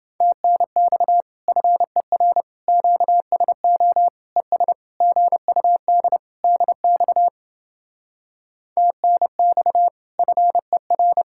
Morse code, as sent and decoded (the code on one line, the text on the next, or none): - -. -..- / ..-. .-. / --.- ... --- / . ... / --. ..- -.. / -.. -..- / - -. -..- / ..-. .-.
TNX FER QSO ES GUD DX TNX FER